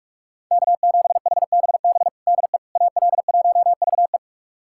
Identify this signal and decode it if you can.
Morse code, heard as a keyed tone, about 47 wpm, 700 Hz. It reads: K7HBB DE AL1VE
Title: USA Callsign DE Callsign - 5 Characters - 45wpm